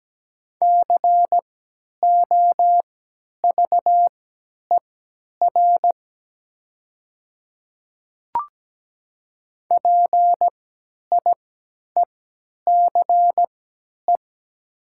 Morse code, heard as a keyed tone, about 17 wpm, spaced as for 10 wpm, 700 Hz. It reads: COVER PIECE
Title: 3x Character Spacing - Top 500 Words - 17wpm